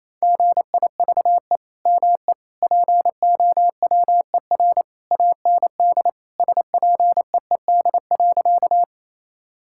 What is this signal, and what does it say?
Morse code, heard as a keyed tone, about 28 wpm, 700 Hz. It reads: GIVE ME POWER AND SPEED.
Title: Sentences from Top 1000 Words - Review Part 5 - 28wpm